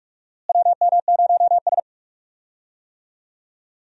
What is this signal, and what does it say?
Morse code, heard as a keyed tone, about 42 wpm, 700 Hz. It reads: WM0S
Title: USA Call Signs - 4 Characters - Repeated 3x - 45wpm